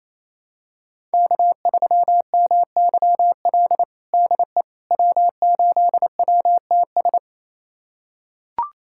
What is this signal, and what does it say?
Morse code, heard as a keyed tone, about 28 wpm, 700 Hz. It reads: K3MYL DE W8WTS
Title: USA Callsign DE Callsign - 5 Characters - 28wpm